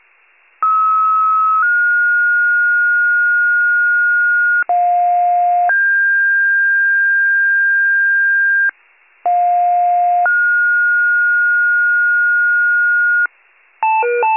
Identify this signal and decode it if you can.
Two-tone page, heard as a one-second tone, then a three-second tone, about 1290 then 1430 Hz, about 710 then 1600 Hz, about 710 then 1430 Hz. It is heard from 0.6 to 13.3 s.
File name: pager / 46 FIRE / wcalert